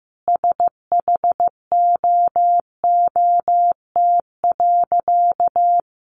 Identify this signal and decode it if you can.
Morse code, heard as a keyed tone, about 15 wpm, 700 Hz. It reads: SHOOT.
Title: A Sound Of Thunder 15wpm